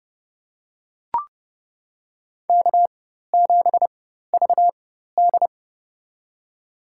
Morse code, heard as a keyed tone, about 32 wpm, spaced as for 14 wpm, 700 Hz. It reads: K7VD
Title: USA Call Signs - 4 Characters - 4x Character Spacing - 30wpm